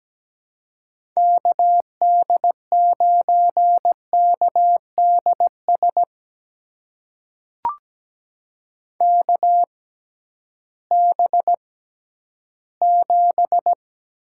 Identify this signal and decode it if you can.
Morse code, heard as a keyed tone, about 17 wpm, spaced as for 11 wpm, 700 Hz. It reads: KD9KDS KB7
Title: USA Call Signs - 6 Characters - 6x Character Spacing - 17wpm